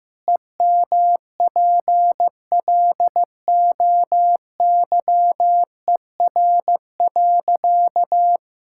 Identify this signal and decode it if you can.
Morse code, heard as a keyed tone, about 15 wpm, 700 Hz. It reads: EMPLOYER.
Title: Pangram Challenge v2 - 15wpm